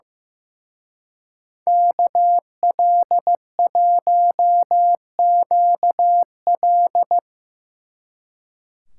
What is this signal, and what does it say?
Morse code, heard as a keyed tone, about 17 wpm, 700 Hz. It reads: KL1QL
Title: Speed Racing - USA 2x2 Call Signs 15wpm